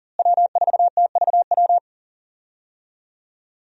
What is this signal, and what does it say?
Morse code, heard as a keyed tone, about 37 wpm, 700 Hz. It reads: W4TVW
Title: USA Call Signs - 5 Characters - Repeated 3x - 40wpm